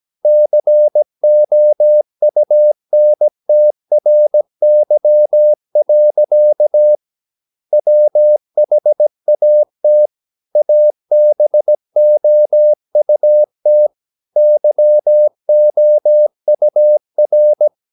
Morse code, heard as a keyed tone, about 17 wpm, 600 Hz. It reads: COUNTRY. WHAT ABOUT YOUR
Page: Romanen Never af Ken Follet på engelsk.